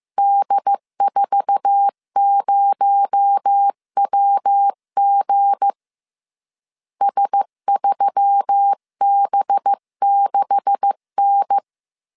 Morse code, heard as a keyed tone, about 15 wpm, 800 Hz. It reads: D40WG S3B6N
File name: CW / DET / 1 9 1500C S B 14wpm 3302